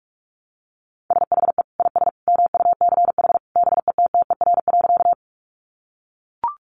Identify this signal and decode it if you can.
Morse code, heard as a keyed tone, about 44 wpm, 700 Hz. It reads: SHE IS MUCH BETTER.
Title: Sentences from Top 400 Words 45wpm